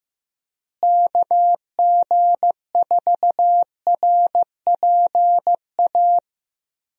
Morse code, heard as a keyed tone, about 15 wpm, 700 Hz. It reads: KG4RPA
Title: USA Call Signs - 6 Characters - Repeated 3x - 15wpm